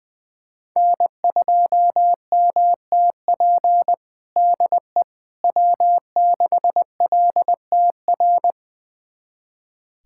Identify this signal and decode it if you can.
Morse code, heard as a keyed tone, about 21 wpm, 700 Hz. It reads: N2MTP DE W6LTR
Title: USA Callsign DE Callsign - 5 Characters - 20wpm